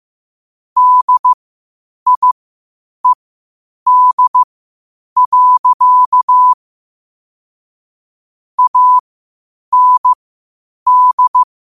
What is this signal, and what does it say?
Morse code, heard as a keyed tone, about 15 wpm, spaced as for 8 wpm, 1000 Hz. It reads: DIED. AND
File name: qotdfiles / 8wpm